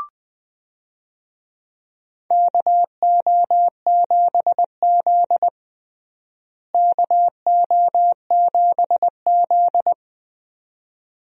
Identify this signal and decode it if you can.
Morse code, heard as a keyed tone, about 20 wpm, 700 Hz. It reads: KO7Z KO7Z
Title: USA Call Signs - 4 Characters - Repeated 2x - 20wpm